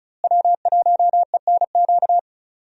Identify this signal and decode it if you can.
Morse code, heard as a keyed tone, about 36 wpm, 700 Hz. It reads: W1ENQ
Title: USA Call Signs - 5 Characters - Repeated 3x - 35wpm